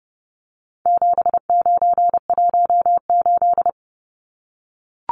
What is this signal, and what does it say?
Morse code, heard as a keyed tone, about 30 wpm, 700 Hz. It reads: 7918
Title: Numbers - 4-Digit 30wpm